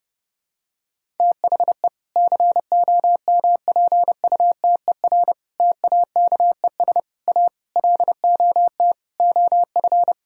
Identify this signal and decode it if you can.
Morse code, heard as a keyed tone, about 30 wpm, 700 Hz. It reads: THE COMPUTER TAKES A LOT OF
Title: Sentences from Top 1000 Words - Review Part 6 - 30wpm